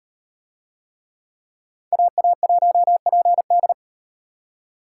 Morse code, heard as a keyed tone, about 38 wpm, 700 Hz. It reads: AA1PD